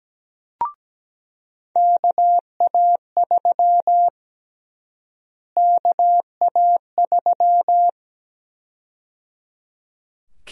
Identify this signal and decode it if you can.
Morse code, heard as a keyed tone, about 17 wpm, 700 Hz. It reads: KA3 KA3
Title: USA Call Sign Prefixes - Repeated 2x - 17wpm